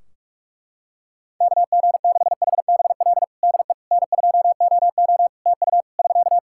Morse code, heard as a keyed tone, about 46 wpm, 700 Hz. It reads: KG6HBL DE N2OO TU <SK>
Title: K1USN SST QSO Practice 45wpm